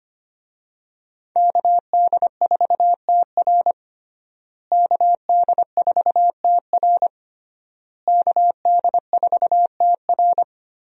Morse code, heard as a keyed tone, about 25 wpm, 700 Hz. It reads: KD4TR KD4TR KD4TR
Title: USA Call Signs - 5 Characters - Repeated 3x - 25wpm